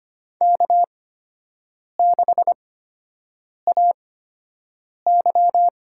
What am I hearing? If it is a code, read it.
Morse code, heard as a keyed tone, about 25 wpm, spaced as for 7 wpm, 700 Hz. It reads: K6AY
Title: USA Call Signs - 8x Character Spacing - 25wpm